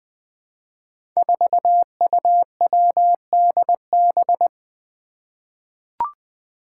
Morse code, heard as a keyed tone, about 20 wpm, 700 Hz. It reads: 4UWDB
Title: Five Challenging Letter-Number Combinations 20wpm